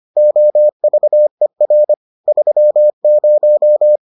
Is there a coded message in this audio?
Yes: Morse code from 0.2 to 4.0 s, heard as a keyed tone, about 25 wpm, 600 Hz. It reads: OVER 30